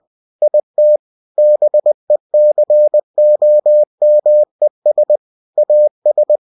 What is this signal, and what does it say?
Morse code, heard as a keyed tone, about 20 wpm, 600 Hz. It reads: IT BECOMES AS
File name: cw-books / winnie-the-pooh / winnie-20wpm-page-0062